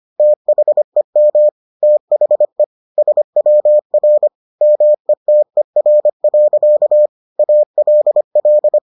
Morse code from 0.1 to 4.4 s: THEM THE SWR

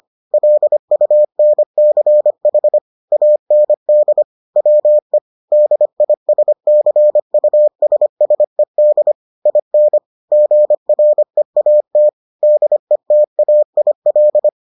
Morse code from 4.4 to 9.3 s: WE DISCUSSED

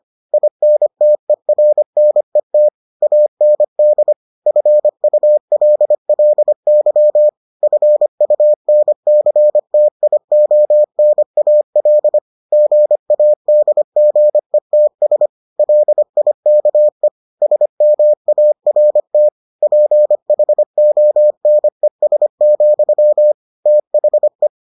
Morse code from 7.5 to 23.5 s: FUNCTIONAL GADGETS LIKE SMART PHONES,

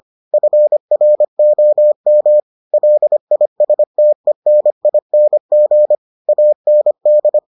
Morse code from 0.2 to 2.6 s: FROM

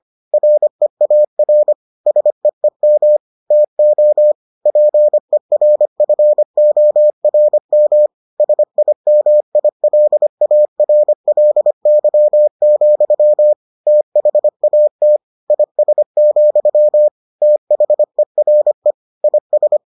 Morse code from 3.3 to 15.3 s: TO PERFORM SIMILARLY, THAT

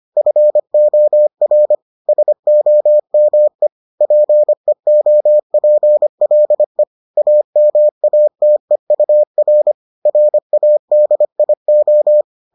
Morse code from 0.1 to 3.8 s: FOR SOME

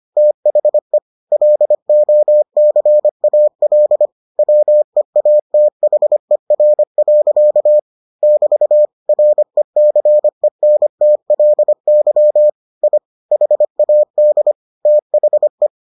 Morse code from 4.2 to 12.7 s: WEATHER. <BT> RECENTLY

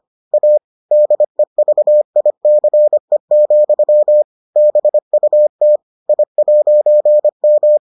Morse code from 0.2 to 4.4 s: A DEVICE,

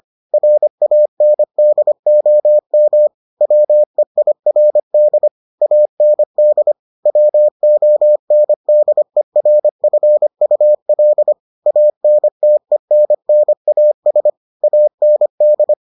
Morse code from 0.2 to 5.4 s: RANDOM WEIRD

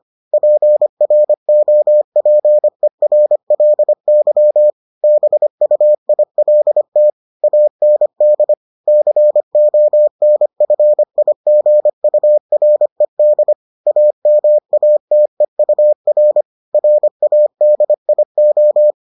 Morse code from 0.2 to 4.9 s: PROPERLY

